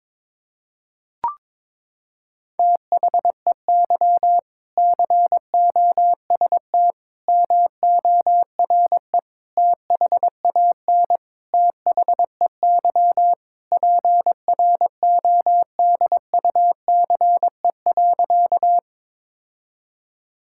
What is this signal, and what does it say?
Morse code, heard as a keyed tone, about 22 wpm, 700 Hz. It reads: THEY COST MORE THAN THEY PRODUCE.
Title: Sentences from Top 1000 Words - Review Part 5 - 22wpm